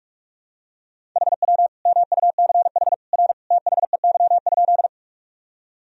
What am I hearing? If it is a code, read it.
Morse code, heard as a keyed tone, about 44 wpm, 700 Hz. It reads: HW MUCH R THEY?
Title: Sentences from Top 200 Words and Abbreviations 45wpm